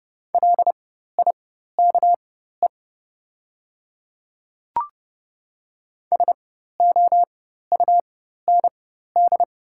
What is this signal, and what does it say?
Morse code, heard as a keyed tone, about 30 wpm, spaced as for 14 wpm, 700 Hz. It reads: LIKE SOUND
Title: 4x Character Spacing - Top 200 Words - 30wpm